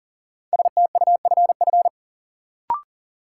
Morse code, heard as a keyed tone, about 40 wpm, 700 Hz. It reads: STUFF
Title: Sentences from Top 1400 Words - New Word Drill Alt - 40wpm